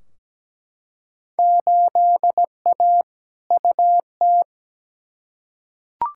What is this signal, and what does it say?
Morse code, heard as a keyed tone, about 17 wpm, 700 Hz. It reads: 8A UT
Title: Field Day Exchanges Encore - 17wpm